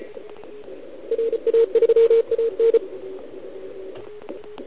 Morse code, heard as a keyed tone, about 35 wpm, 460 Hz. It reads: RA3AN
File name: ARCHIV / OKDX2020 / OL4Wa